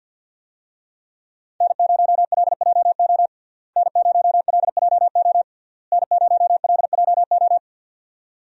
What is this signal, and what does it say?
Morse code, heard as a keyed tone, about 50 wpm, 700 Hz. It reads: N0LJO N0LJO N0LJO
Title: USA Call Signs - 5 Characters - Repeated 3x - 50wpm